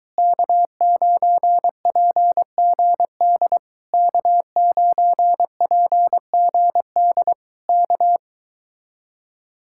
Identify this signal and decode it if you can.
Morse code, heard as a keyed tone, about 23 wpm, 700 Hz. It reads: K9PGD K9PGD K